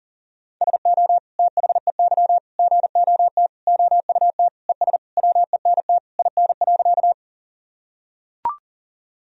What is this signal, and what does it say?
Morse code, heard as a keyed tone, about 40 wpm, 700 Hz. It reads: SO THEY GOT OUT ES WENT IN.
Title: Sentences from Top 300 Words and Abbreviations 40wpm